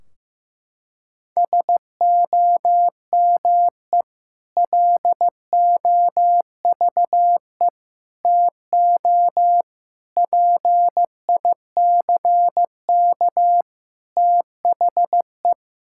Morse code, heard as a keyed tone, about 14 wpm, 700 Hz. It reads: SOME LOVE TO PICK THE
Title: Sentences from Top 1600 Words 15wpm